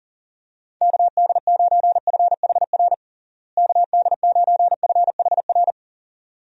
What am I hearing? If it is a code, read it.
Morse code, heard as a keyed tone, about 40 wpm, 700 Hz. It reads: KD9FHR KD9FHR
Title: USA Call Signs - 6 Characters - Repeated 2x - 40wpm